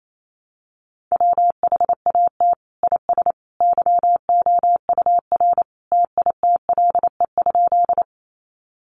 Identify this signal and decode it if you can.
Morse code, heard as a keyed tone, about 28 wpm, 700 Hz. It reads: WHAT IS YOUR TITLE?
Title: Sentences from Top 1200 Words 28wpm